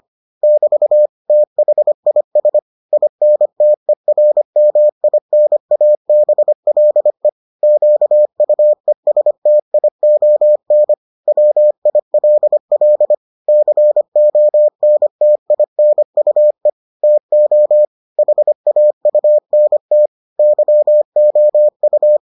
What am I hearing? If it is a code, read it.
Morse code, heard as a keyed tone, about 25 wpm, 600 Hz. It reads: <BT> THIS INTERMINABLE QUESTION WILL CONTINUE TO HAUNT YOU